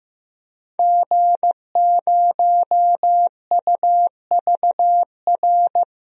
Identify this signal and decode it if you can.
Morse code, heard as a keyed tone, about 16 wpm, 700 Hz. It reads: G0UVR
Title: DXCC-100 Call Signs - 5 Characters - Repeated 3x - 15wpm